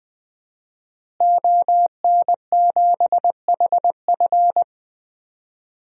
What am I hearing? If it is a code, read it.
Morse code, heard as a keyed tone, about 20 wpm, 700 Hz. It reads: ON7HF
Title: DXCC-100 Call Signs - 5 Characters - Repeated 3x - 20wpm